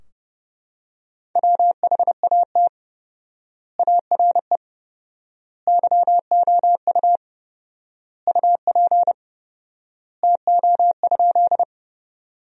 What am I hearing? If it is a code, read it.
Morse code, heard as a keyed tone, about 30 wpm, 700 Hz. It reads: WHAT ARE YOU UP TO?
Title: 4x Word-Spacing - Sentences from Top 100 Words - 30wpm